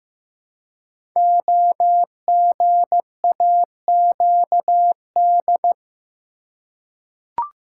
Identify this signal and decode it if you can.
Morse code, heard as a keyed tone, about 15 wpm, 700 Hz. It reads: OGAQD
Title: Code Groups - 5 Letters - 15wpm